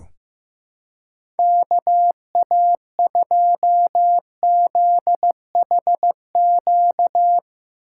Morse code, heard as a keyed tone, about 15 wpm, 700 Hz. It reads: KA2ZHQ
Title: USA Call Signs - 2x Character Spacing - 15wpm